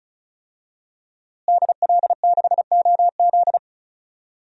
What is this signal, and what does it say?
Morse code, heard as a keyed tone, about 34 wpm, 700 Hz. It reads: DL6OZ
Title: DXCC-100 Call Signs - 5 Characters - Repeated 2x - 35wpm